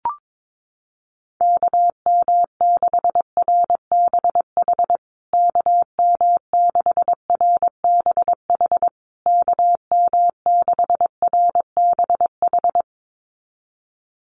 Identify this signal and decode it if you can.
Morse code, heard as a keyed tone, about 22 wpm, 700 Hz. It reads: KM6RBH KM6RBH KM6RBH
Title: USA Call Signs Repeated 3x 22wpm